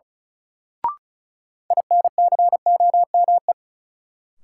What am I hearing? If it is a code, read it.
Morse code, heard as a keyed tone, about 36 wpm, 700 Hz. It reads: INCOME
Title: Sentences from Top 1000 Words - Review Part 5 - New Word Drill - 35wpm